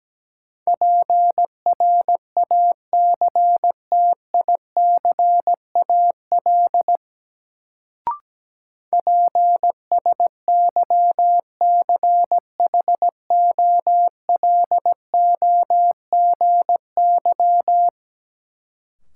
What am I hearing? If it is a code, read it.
Morse code, heard as a keyed tone, about 17 wpm, 700 Hz. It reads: PRACTICAL PSYCHOLOGY
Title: Sentences from Top 1400 Words - New Word Drill - 17wpm